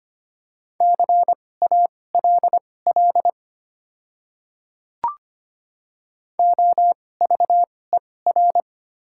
Morse code, heard as a keyed tone, about 25 wpm, spaced as for 18 wpm, 700 Hz. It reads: CALL OVER
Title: 2x Character Spacing - Top 100 Words - 25wpm